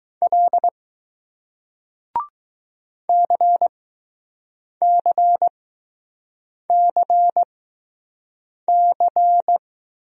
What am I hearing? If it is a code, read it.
Morse code, heard as a keyed tone, about 19 wpm, spaced as for 6 wpm, 700 Hz. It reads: L CCCC